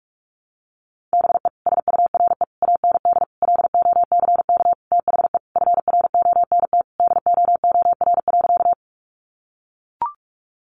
Morse code, heard as a keyed tone, about 44 wpm, 700 Hz. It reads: BE SURE AND LOCK THE FRONT DOOR.
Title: Sentences from Top 1300 Words 45wpm